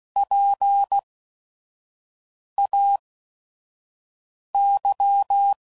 Morse code, heard as a keyed tone, about 16 wpm, spaced as for 5 wpm, 800 Hz. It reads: PAY
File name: Prefixes / By First Letter / Ps